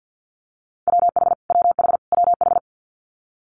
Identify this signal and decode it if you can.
Morse code, heard as a keyed tone, about 50 wpm, 700 Hz. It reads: WH WH WH
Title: Multi-letter Phonemes Repeated 3x 50wpm